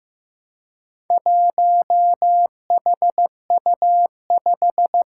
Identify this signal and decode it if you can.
Morse code, heard as a keyed tone, about 15 wpm, 700 Hz. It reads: 1HU5
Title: Four Challenging Letter-Number Combinations 15wpm